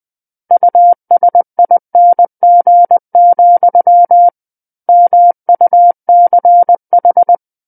Morse code, heard as a keyed tone, about 20 wpm, 700 Hz. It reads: USING, MUCH
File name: 20 WPM